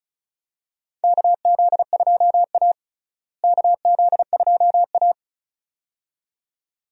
Morse code, heard as a keyed tone, about 36 wpm, 700 Hz. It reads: KZ2A KZ2A
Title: USA Call Signs - 4 Characters - Repeated 2x - 35wpm